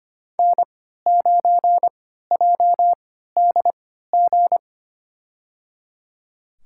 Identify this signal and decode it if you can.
Morse code, heard as a keyed tone, about 25 wpm, spaced as for 14 wpm, 700 Hz. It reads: N9JDG
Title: USA Call Signs - 5 Characters - 3x Character Spacing - 25wpm